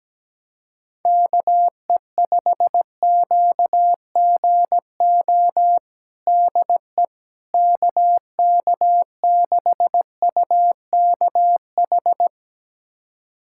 Morse code, heard as a keyed tone, about 17 wpm, 700 Hz. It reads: KE5QGO DE KK6UKH
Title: USA Callsign DE Callsign - 6 Characters - 17wpm